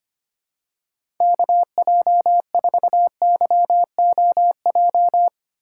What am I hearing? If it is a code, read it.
Morse code, heard as a keyed tone, about 25 wpm, 700 Hz. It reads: KJ4YOJ
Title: USA Call Signs - 6 Characters - 2x Character Spacing - 25wpm